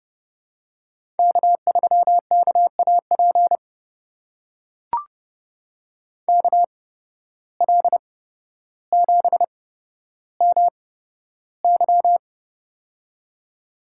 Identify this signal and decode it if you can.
Morse code, heard as a keyed tone, about 31 wpm, spaced as for 13 wpm, 700 Hz. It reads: K3KAP KL7MY
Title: USA Call Signs - 5 Characters - 8x Character Spacing - 30wpm